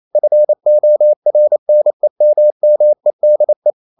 Morse code, heard as a keyed tone, about 28 wpm, 600 Hz. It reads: FORNEMMEDE
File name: Ferie QRQ 28wpm